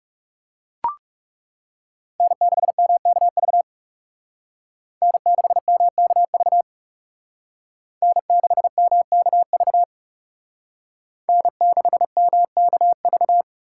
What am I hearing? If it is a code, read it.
Morse code, heard as a keyed tone, about 38 wpm, 700 Hz. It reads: N6MKV N6MKV N6MKV N6MKV